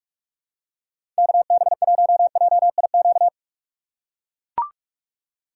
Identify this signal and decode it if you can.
Morse code, heard as a keyed tone, about 42 wpm, 700 Hz. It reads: KB1JIQ
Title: USA Call Signs - 6 Characters - 6x Character Spacing - 45wpm